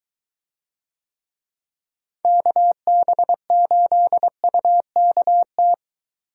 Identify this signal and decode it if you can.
Morse code, heard as a keyed tone, about 22 wpm, 700 Hz. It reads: KB8UKT